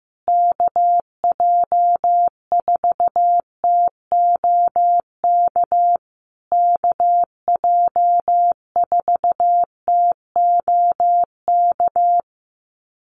Morse code, heard as a keyed tone, about 15 wpm, 700 Hz. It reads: KJ4TOK KJ4TOK
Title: USA Call Signs Repeated 3x 15wpm